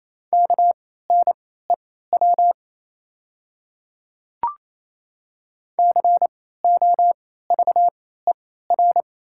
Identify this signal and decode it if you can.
Morse code, heard as a keyed tone, about 28 wpm, spaced as for 16 wpm, 700 Hz. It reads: KNEW COVER
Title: 3x Character Spacing - Top 500 Words - 28wpm